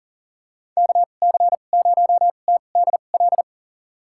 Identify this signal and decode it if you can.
Morse code, heard as a keyed tone, about 40 wpm, spaced as for 29 wpm, 700 Hz. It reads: KC0TDL
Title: USA Call Signs - 6 Characters - 2x Character Spacing - 40wpm